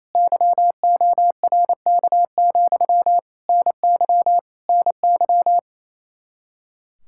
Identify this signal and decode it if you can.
Morse code, heard as a keyed tone, about 28 wpm, 700 Hz. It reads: YORK, NY NY
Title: Practice QSO - Exchange 1 - Beginning - 28wpm